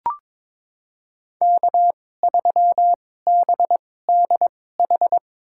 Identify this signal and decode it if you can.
Morse code, heard as a keyed tone, about 22 wpm, spaced as for 16 wpm, 700 Hz. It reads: K3BDH
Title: USA Call Signs - 5 Characters - 2x Character Spacing - 22wpm